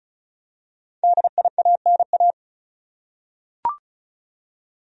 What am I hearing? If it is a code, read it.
Morse code, heard as a keyed tone, about 36 wpm, 700 Hz. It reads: DIANA
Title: USA Common Names - Max 5 Letters - 35wpm